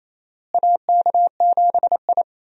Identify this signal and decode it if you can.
Morse code, heard as a keyed tone, about 28 wpm, 700 Hz. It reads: AK7I
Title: USA Call Signs - 4 Characters - Repeated 2x - 28wpm